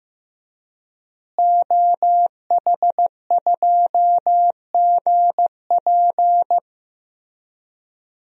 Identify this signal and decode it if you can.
Morse code, heard as a keyed tone, about 15 wpm, 700 Hz. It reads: OH2GP
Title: DXCC-100 Call Signs - 5 Characters - Repeated 2x - 15wpm